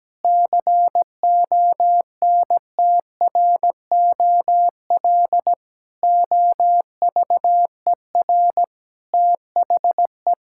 Morse code, heard as a keyed tone, about 17 wpm, 700 Hz. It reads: CONTROL OVER THE
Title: Sentences from Top 1000 Words - Review Part 5 - 17wpm